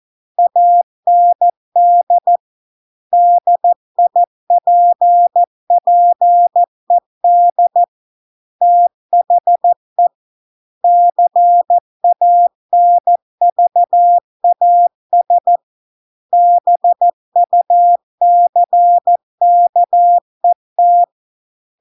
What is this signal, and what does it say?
Morse code, heard as a keyed tone, about 14 wpm, 700 Hz. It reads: AND DIPPED THE CANVAS BUCKET